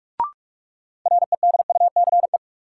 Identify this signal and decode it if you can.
Morse code, heard as a keyed tone, about 47 wpm, 700 Hz. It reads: REDUCE
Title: Sentences from Top 1100 Words - New Word Drill - 45wpm